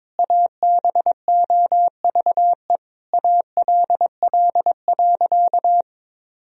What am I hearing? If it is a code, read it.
Morse code, heard as a keyed tone, about 22 wpm, 700 Hz. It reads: ABOVE ALL.
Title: Sentences from Top 1000 Words - Review Part 2 - 22wpm